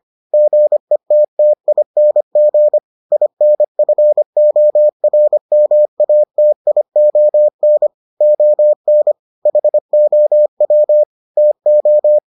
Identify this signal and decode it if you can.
Morse code, heard as a keyed tone, about 25 wpm, 600 Hz. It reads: GETTING INFORMATION ON HOW TO